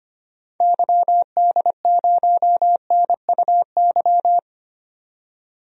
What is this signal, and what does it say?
Morse code, heard as a keyed tone, about 25 wpm, 700 Hz. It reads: YD0NUY
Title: DXCC-100 Call Signs - 6 Characters - 25wpm